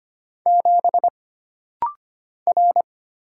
Morse code, heard as a keyed tone, about 25 wpm, 700 Hz. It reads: 7 R
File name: Single Character (Letter:Number:Punctuation:Prosign) - Rapid-Fire - 25wpm